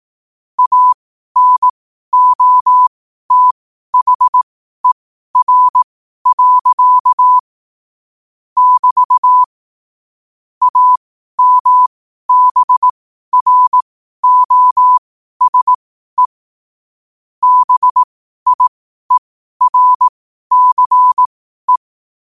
Morse code, heard as a keyed tone, about 18 wpm, spaced as for 12 wpm, 1000 Hz. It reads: ANOTHER. <BT> AMBROSE BIERCE
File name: qotdfiles / 12wpm